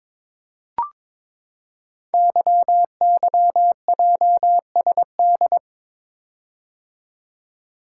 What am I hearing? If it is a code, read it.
Morse code, heard as a keyed tone, about 22 wpm, 690 Hz. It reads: YYJSD
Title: Code Groups - 5 Letters - 22wpm